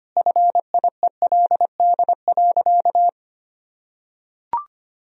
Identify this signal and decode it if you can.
Morse code, heard as a keyed tone, about 25 wpm, 700 Hz. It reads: FIELD.
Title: Sentences from Top 1000 Words - Review Part 2 - 25wpm